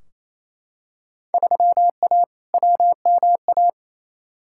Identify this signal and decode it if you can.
Morse code, heard as a keyed tone, about 28 wpm, 700 Hz. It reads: 3A WMA
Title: Field Day Exchanges Encore - 28wpm